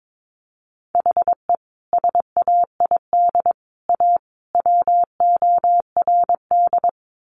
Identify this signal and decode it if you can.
Morse code, heard as a keyed tone, about 22 wpm, 700 Hz. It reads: HE SAID A WORD
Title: Sentences from Top 600 Words 22wpm